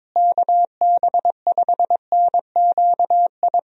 Morse code, heard as a keyed tone, about 22 wpm, 700 Hz. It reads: KB5NQI
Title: USA Call Signs - 6 Characters - Repeated 2x - 22wpm